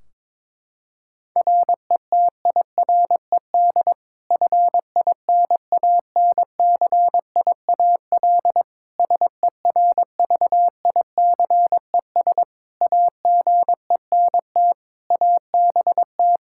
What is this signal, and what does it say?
Morse code, heard as a keyed tone, about 22 wpm, 700 Hz. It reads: RETIRED FINANCIAL SERVICES AGENT ABT
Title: K1USN SST QSO Practice 22wpm